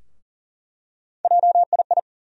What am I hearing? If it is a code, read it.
Morse code, heard as a keyed tone, about 40 wpm, 700 Hz. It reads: JII
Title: USA Call Sign Postfixes - 3 Letters 40wpm